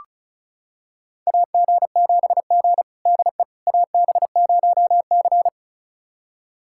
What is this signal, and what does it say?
Morse code, heard as a keyed tone, about 36 wpm, 700 Hz. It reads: AG7G DE AB0C
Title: USA Callsign DE Callsign - 4 Characters - 35wpm